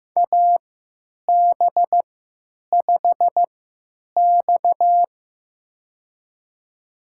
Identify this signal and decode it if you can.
Morse code, heard as a keyed tone, about 15 wpm, spaced as for 9 wpm, 700 Hz. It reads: AB5X
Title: USA Call Signs - 4 Characters - 3x Character Spacing - 15wpm